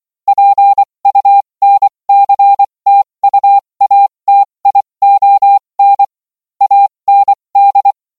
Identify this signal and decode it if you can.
Morse code, heard as a keyed tone, about 24 wpm, spaced as for 20 wpm, 800 Hz. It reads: PUNCTUATION AND